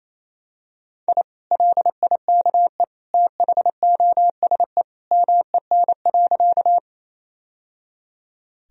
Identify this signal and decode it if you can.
Morse code, heard as a keyed tone, about 28 wpm, 700 Hz. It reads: I LIKE THOSE MEN.
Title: Sentences from Top 1000 Words - Review Part 1 - 28wpm